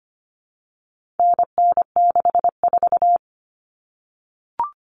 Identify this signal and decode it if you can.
Morse code, heard as a keyed tone, about 25 wpm, 700 Hz. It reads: NN64
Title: Common Cut Numbers - 4-Digit 25wpm